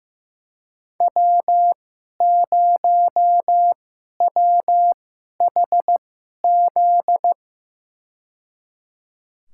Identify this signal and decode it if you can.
Morse code, heard as a keyed tone, about 15 wpm, spaced as for 11 wpm, 700 Hz. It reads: W0WHZ
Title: USA Call Signs - 5 Characters - 2x Character Spacing - 15wpm